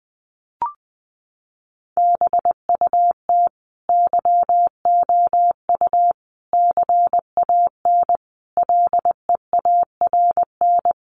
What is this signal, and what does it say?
Morse code, heard as a keyed tone, about 20 wpm, 700 Hz. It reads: BUT YOU CAN LEARN
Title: Sentences from Top 1500 Words 20wpm